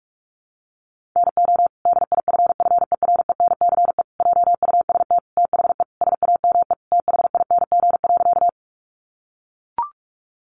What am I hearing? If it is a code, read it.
Morse code, heard as a keyed tone, about 44 wpm, 700 Hz. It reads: NO DIFFERENCE JUST THE SAME THING.
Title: Sentences from Top 800 Words 45wpm